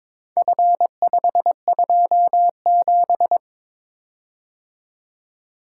Morse code, heard as a keyed tone, about 22 wpm, 700 Hz. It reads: F527
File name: practice / 22wpm / Four Letter-Number Combinations - 22wpm